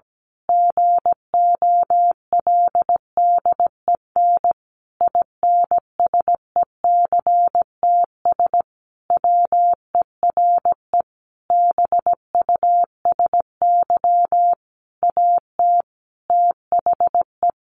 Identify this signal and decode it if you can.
Morse code, heard as a keyed tone, about 17 wpm, 700 Hz. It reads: GOLDEN INSECTS WERE BUSY AT THE
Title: A Sound Of Thunder 17wpm